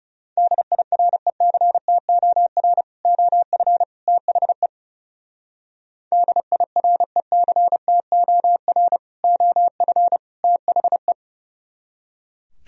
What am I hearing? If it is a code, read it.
Morse code, heard as a keyed tone, about 34 wpm, 700 Hz. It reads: DIRECTOR OF THE DIRECTOR OF THE
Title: Speed-Racing - Sets of 3 Words 30wpm